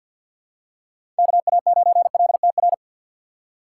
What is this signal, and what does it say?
Morse code, heard as a keyed tone, about 52 wpm, 700 Hz. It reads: KA9LTR
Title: USA Call Signs - 6 Characters - 4x Character Spacing - 50wpm